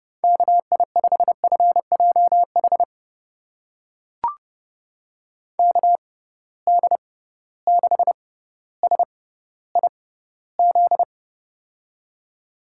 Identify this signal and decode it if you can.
Morse code, heard as a keyed tone, about 30 wpm, spaced as for 15 wpm, 700 Hz. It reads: KI5FJH KD6SIZ
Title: USA Call Signs - 6 Characters - 6x Character Spacing - 30wpm